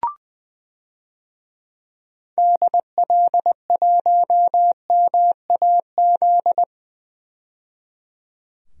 Morse code, heard as a keyed tone, about 20 wpm, 700 Hz. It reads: DL1MAZ
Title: DXCC-100 Call Signs - 6 Characters - 20wpm